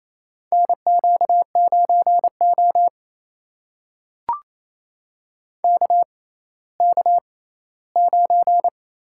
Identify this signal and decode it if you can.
Morse code, heard as a keyed tone, about 28 wpm, spaced as for 16 wpm, 700 Hz. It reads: NQ9O KK9
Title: USA Call Signs - 4 Characters - 6x Character Spacing - 28wpm